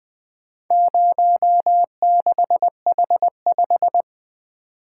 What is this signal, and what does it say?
Morse code, heard as a keyed tone, about 20 wpm, 700 Hz. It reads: 06H5
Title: Four Letter-Number Combinations - 20wpm